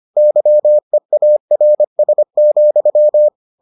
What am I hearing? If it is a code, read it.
Morse code, heard as a keyed tone, about 25 wpm, 600 Hz. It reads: YEARS,